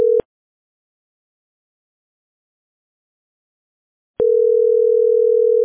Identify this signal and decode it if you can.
Ringback tone (North America), 440 + 480 Hz.